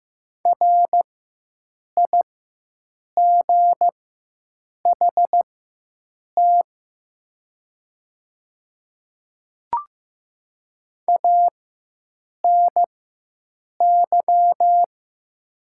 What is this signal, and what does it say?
Morse code, heard as a keyed tone, about 15 wpm, spaced as for 7 wpm, 700 Hz. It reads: RIGHT ANY
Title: 4x Character Spacing - Top 300 Words - 15wpm